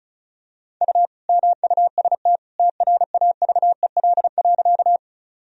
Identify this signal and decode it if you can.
Morse code, heard as a keyed tone, about 36 wpm, 700 Hz. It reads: U MUST TRAVEL.
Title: Sentences from Top 400 Words and Abbreviations 35wpm